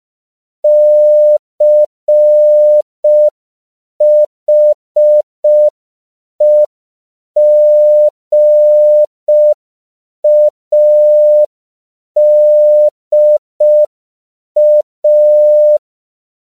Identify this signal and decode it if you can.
Morse code, heard as a keyed tone, about 5 wpm, 600 Hz. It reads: CHEGADA
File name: 5PPM